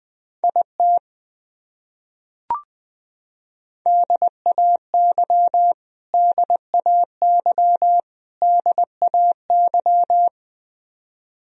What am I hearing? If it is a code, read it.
Morse code, heard as a keyed tone, about 20 wpm, 700 Hz. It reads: IT DAY DAY DAY
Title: Top 100 Words Repeated 20wpm